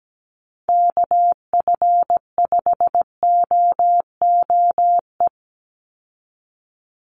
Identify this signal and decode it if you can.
Morse code, heard as a keyed tone, about 17 wpm, 700 Hz. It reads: KF5OOE
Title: USA Call Signs 17wpm